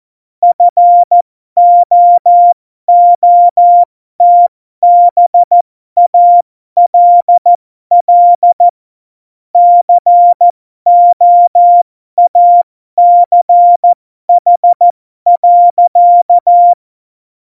Morse code, heard as a keyed tone, about 14 wpm, spaced as for 12 wpm, 700 Hz. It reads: FOOTBALL COACH.
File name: Morsekurs / Witze